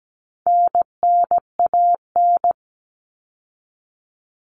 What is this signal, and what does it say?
Morse code, heard as a keyed tone, about 17 wpm, 700 Hz. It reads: NNAN
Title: All Cut Numbers - 4-Digit 17wpm